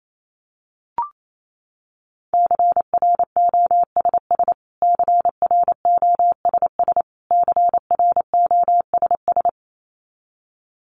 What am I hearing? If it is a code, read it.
Morse code, heard as a keyed tone, about 28 wpm, 700 Hz. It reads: CROSS CROSS CROSS
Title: Top 500 Words Repeated 28wpm